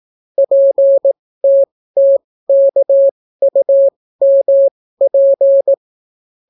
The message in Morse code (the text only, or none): PTTKUMP